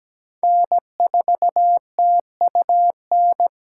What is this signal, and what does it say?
Morse code, heard as a keyed tone, about 17 wpm, 700 Hz. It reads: N4TUN
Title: USA Call Signs - 5 Characters - 2x Character Spacing - 17wpm